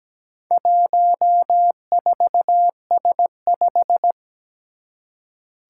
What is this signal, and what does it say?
Morse code, heard as a keyed tone, about 17 wpm, 700 Hz. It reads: 14S5
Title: Four Challenging Letter-Number Combinations 17wpm